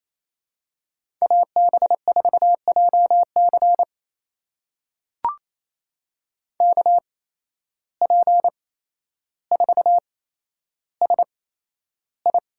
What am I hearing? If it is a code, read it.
Morse code, heard as a keyed tone, about 28 wpm, spaced as for 12 wpm, 700 Hz. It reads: AB4JC KP4SI
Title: USA Call Signs - 5 Characters - 8x Character Spacing - 28wpm